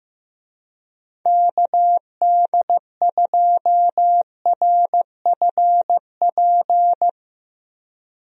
Morse code, heard as a keyed tone, about 15 wpm, 700 Hz. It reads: KD2RFP
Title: USA Call Signs - 6 Characters - Repeated 3x - 15wpm